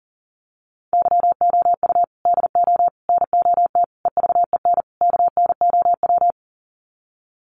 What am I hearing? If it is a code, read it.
Morse code, heard as a keyed tone, about 40 wpm, 700 Hz. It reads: YOU DO NOT EVEN KNOW
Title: Sets of 5 Words Encore2 40wpm